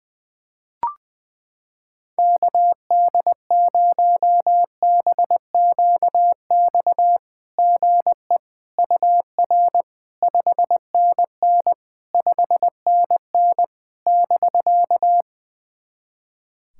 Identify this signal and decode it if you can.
Morse code, heard as a keyed tone, about 20 wpm, 690 Hz. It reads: KD0BQX GE UR 5NN 5NN <BK>
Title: SOTA QSO Practice 20wpm